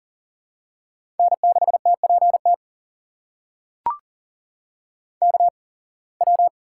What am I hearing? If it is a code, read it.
Morse code, heard as a keyed tone, about 40 wpm, spaced as for 26 wpm, 700 Hz. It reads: N6TPT KW
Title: USA Call Signs - 5 Characters - 8x Character Spacing - 40wpm